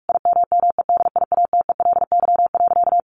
Morse code, heard as a keyed tone, about 44 wpm, 700 Hz. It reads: IMMEDIATELY.
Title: Sentences from Top 1100 Words 45wpm